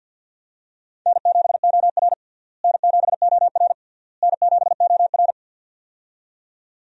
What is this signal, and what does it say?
Morse code, heard as a keyed tone, about 52 wpm, 700 Hz. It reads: N7OR N7OR N7OR
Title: USA Call Signs - 4 Characters - Repeated 3x - 50wpm